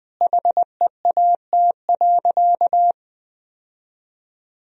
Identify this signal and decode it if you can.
Morse code, heard as a keyed tone, about 20 wpm, 700 Hz. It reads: HEAT.
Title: Sentences from Top 1000 Words - Review Part 3 - 20wpm